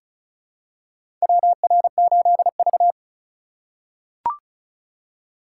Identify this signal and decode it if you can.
Morse code, heard as a keyed tone, about 36 wpm, 700 Hz. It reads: WR8V
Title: USA Call Signs - 4 Characters - 3x Character Spacing - 35wpm